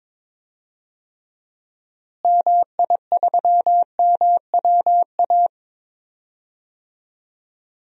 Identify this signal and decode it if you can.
Morse code, heard as a keyed tone, about 21 wpm, 700 Hz. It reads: MI3MWA
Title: DXCC-100 Call Signs - 6 Characters - Repeated 3x - 22wpm